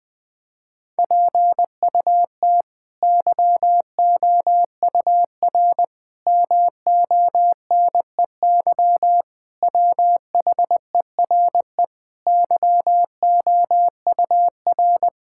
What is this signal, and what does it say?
Morse code, heard as a keyed tone, about 21 wpm, 700 Hz. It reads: PUT YOUR MONEY WHERE YOUR
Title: Sentences with 8-Word Phrases - Spoken Beginning - 20wpm